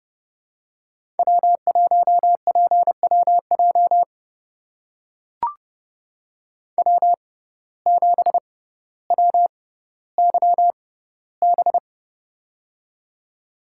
Morse code, heard as a keyed tone, about 30 wpm, spaced as for 15 wpm, 700 Hz. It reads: W1PWJ W7WYB
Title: USA Call Signs - 5 Characters - 6x Character Spacing - 30wpm